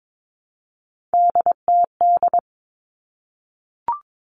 Morse code, heard as a keyed tone, about 22 wpm, 700 Hz. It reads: DTD
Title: All Cut Numbers - 3-Digit 22wpm